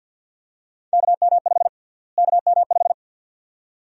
Morse code, heard as a keyed tone, about 53 wpm, 700 Hz. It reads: KM5 KM5
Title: USA Call Sign Prefixes - Repeated 2x - 50wpm